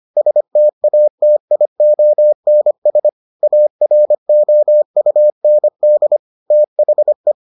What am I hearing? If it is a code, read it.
Morse code, heard as a keyed tone, about 25 wpm, 600 Hz. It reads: STATIONS AROUND THE